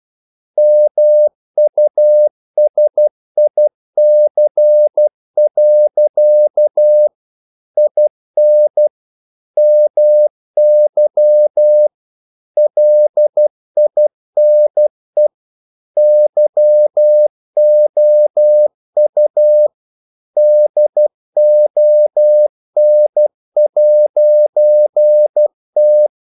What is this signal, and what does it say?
Morse code, heard as a keyed tone, about 12 wpm, 600 Hz. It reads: MUSIC. IN MY LINE YOU DON'T
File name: CW / Practice / EnglishQuotations0312